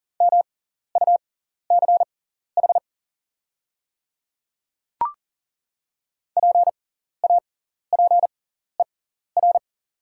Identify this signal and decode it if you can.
Morse code, heard as a keyed tone, about 39 wpm, spaced as for 14 wpm, 690 Hz. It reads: MUCH PAPER
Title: 6x Character Spacing - Top 500 Words - 40wpm